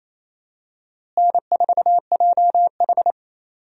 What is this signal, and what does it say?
Morse code, heard as a keyed tone, about 28 wpm, 700 Hz. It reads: N4JH
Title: USA Call Signs - 4 Characters - Repeated 3x - 28wpm